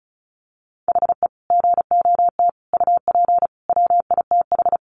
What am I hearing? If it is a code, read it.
Morse code, heard as a keyed tone, about 34 wpm, 700 Hz. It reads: HE GOT UP WITH
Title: Sentences from Top 1200 Words 35wpm